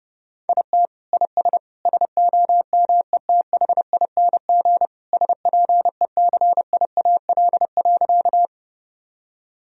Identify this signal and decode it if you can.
Morse code, heard as a keyed tone, about 30 wpm, 700 Hz. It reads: IT IS SOMETHING SPECIAL.
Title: Sentences from Top 1000 Words - Review Part 3 - 30wpm